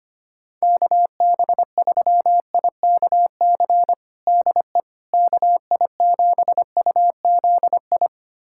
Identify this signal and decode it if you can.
Morse code, heard as a keyed tone, about 25 wpm, 700 Hz. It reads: KB3IKC DE KI7UZI
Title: USA Callsign DE Callsign - 6 Characters - 25wpm